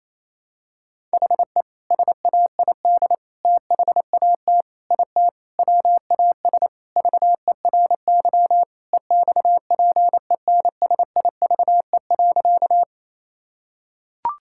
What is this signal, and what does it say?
Morse code, heard as a keyed tone, about 28 wpm, 700 Hz. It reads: HE SAID THAT IT WAS VERY EXPENSIVE.
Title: Sentences from Top 1000 Words - Review Part 6 - 28wpm